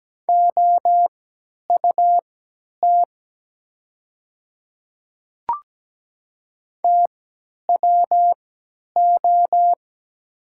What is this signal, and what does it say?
Morse code, heard as a keyed tone, about 17 wpm, spaced as for 10 wpm, 700 Hz. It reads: OUT TWO
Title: 3x Character Spacing - Top 100 Words - 17wpm